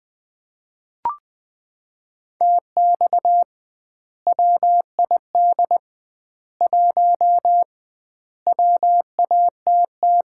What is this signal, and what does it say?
Morse code, heard as a keyed tone, about 20 wpm, 700 Hz. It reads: TX WID 1 WATT
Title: Common QSO Phrases - 2x Word-Spacing - 20wpm